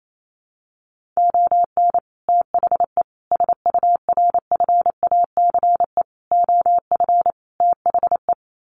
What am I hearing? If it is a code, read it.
Morse code, heard as a keyed tone, about 28 wpm, 700 Hz. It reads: ON THE SURFACE OF THE
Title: Sets of 5 Words Encore 28wpm